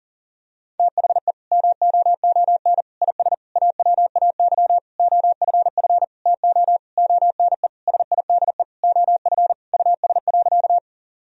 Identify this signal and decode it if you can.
Morse code, heard as a keyed tone, about 40 wpm, 700 Hz. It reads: THE MOON IS AWAY OFF TO ONE SIDE OF US.
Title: Sentences from Top 1000 Words - Review Part 3 - 40wpm